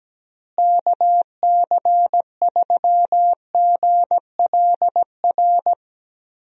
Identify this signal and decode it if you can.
Morse code, heard as a keyed tone, about 17 wpm, 700 Hz. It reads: KC3GLR
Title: USA Call Signs - 6 Characters - Repeated 3x - 17wpm